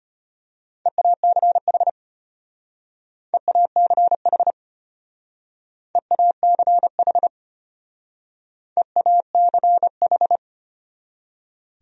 Morse code, heard as a keyed tone, about 30 wpm, 700 Hz. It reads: EACH EACH EACH EACH
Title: Speed-Racing Top 100 Words 25wpm